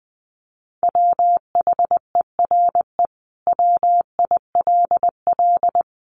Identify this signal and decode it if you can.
Morse code, heard as a keyed tone, about 20 wpm, 700 Hz. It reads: WHERE WILL
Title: Sentences from Top 1300 Words 20wpm